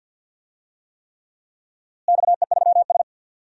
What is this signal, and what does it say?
Morse code, heard as a keyed tone, about 50 wpm, 700 Hz. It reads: XE3S
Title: DXCC-100 Call Signs - 4 Characters - Repeated 3x - 50wpm